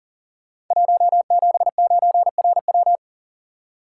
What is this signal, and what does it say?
Morse code, heard as a keyed tone, about 39 wpm, 700 Hz. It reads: 179RW
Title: Code Groups - 5 Letter:Numbers - 40wpm